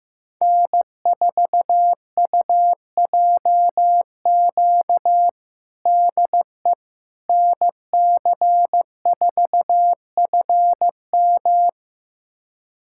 Morse code, heard as a keyed tone, about 16 wpm, 700 Hz. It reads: N4UJQ DE NC4FM
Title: USA Callsign DE Callsign - 5 Characters - 15wpm